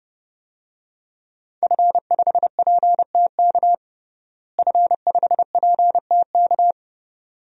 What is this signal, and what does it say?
Morse code, heard as a keyed tone, about 32 wpm, 700 Hz. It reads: F5PTK F5PTK
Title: DXCC-100 Call Signs - 5 Characters - Repeated 2x - 30wpm